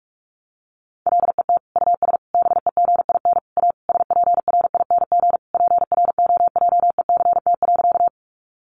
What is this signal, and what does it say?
Morse code, heard as a keyed tone, about 44 wpm, 700 Hz. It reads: LET US BEGIN A SPRING PROJECT.
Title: Sentences from Top 1400 Words 45wpm